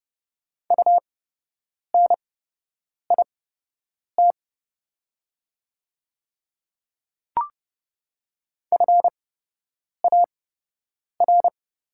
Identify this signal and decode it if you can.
Morse code, heard as a keyed tone, about 29 wpm, spaced as for 8 wpm, 700 Hz. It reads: UNIT FAR
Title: 8x Character Spacing - Top 500 Words - 30wpm